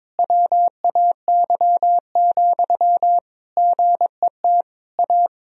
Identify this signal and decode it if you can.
Morse code, heard as a keyed tone, about 22 wpm, 700 Hz. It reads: WAY, GET A
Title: Sentences from Top 1000 Words - Review Part 1 - 22wpm